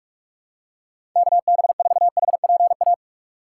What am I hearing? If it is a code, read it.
Morse code, heard as a keyed tone, about 42 wpm, 700 Hz. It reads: KB4HPA
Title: USA Call Signs - 4x Character Spacing - 45wpm